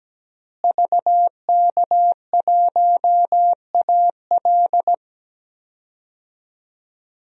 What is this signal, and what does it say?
Morse code, heard as a keyed tone, about 17 wpm, 700 Hz. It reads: VK1AL
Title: DXCC-100 Call Signs - 5 Characters - Repeated 3x - 17wpm